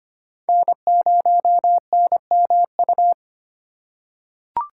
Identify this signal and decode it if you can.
Morse code, heard as a keyed tone, about 25 wpm, 700 Hz. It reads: N0NMU
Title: USA Call Signs - 5 Characters - 3x Character Spacing - 25wpm